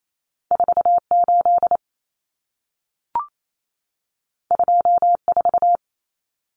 Morse code, heard as a keyed tone, about 28 wpm, 700 Hz. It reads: 48 24
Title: Numbers - 2-Digit 28wpm